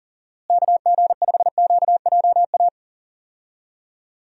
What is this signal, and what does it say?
Morse code, heard as a keyed tone, about 39 wpm, 700 Hz. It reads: KG5QJA
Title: USA Call Signs - 6 Characters - Repeated 3x - 40wpm